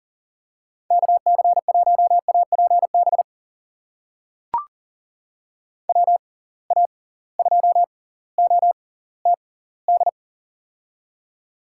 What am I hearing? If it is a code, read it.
Morse code, heard as a keyed tone, about 40 wpm, spaced as for 20 wpm, 700 Hz. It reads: KC1APB WA2OTD
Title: USA Call Signs - 6 Characters - 6x Character Spacing - 40wpm